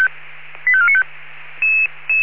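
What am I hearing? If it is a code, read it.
Five-tone selective call (ZVEI), heard as quick runs of five tones, tones of about 70 ms. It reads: 85495
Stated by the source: ZVEI five-tone call 85492 received on SDR.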